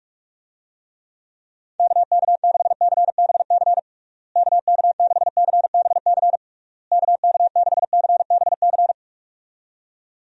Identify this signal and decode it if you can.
Morse code, heard as a keyed tone, about 41 wpm, 700 Hz. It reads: KK6CBC KK6CBC KK6CBC
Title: USA Call Signs - 6 Characters - Repeated 3x - 45wpm